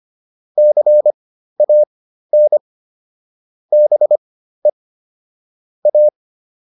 Morse code, heard as a keyed tone, about 25 wpm, spaced as for 13 wpm, 600 Hz. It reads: CAN BE A